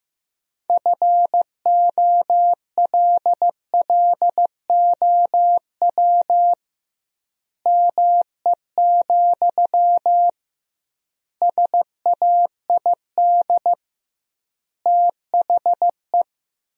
Morse code, heard as a keyed tone, about 15 wpm, 700 Hz. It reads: FOLLOW ME, SAID THE
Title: 2x Word-Spacing - Sentences from Top 200 Words - 15wpm